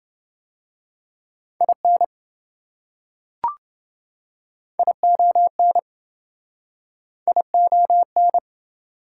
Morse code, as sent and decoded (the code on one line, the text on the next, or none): .. -. / .. --- -. / .. --- -.
IN ION ION